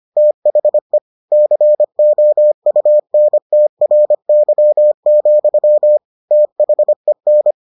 Morse code, heard as a keyed tone, about 25 wpm, 600 Hz. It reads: THE COUNTRY, THEN